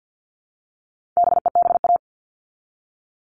Morse code, heard as a keyed tone, about 50 wpm, 700 Hz. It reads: BEBA
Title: All Cut Numbers - 4-Digit 50wpm